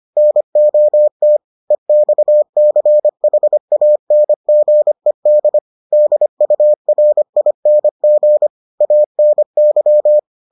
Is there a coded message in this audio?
Yes: Morse code from 0.2 to 10.2 s, heard as a keyed tone, about 25 wpm, 600 Hz. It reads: NOT EXCHANGED DURING ANY